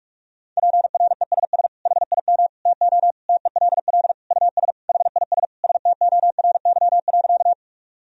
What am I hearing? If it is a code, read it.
Morse code, heard as a keyed tone, about 41 wpm, 700 Hz. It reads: PRESS HIM TO TELL US HIS STORY.
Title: Sentences from Top 1000 Words - Review Part 1 - 45wpm